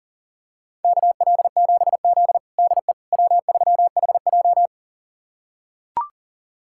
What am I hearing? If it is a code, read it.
Morse code, heard as a keyed tone, about 40 wpm, 700 Hz. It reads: KL7Z DE W3HJ
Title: USA Callsign DE Callsign - 4 Characters - 40wpm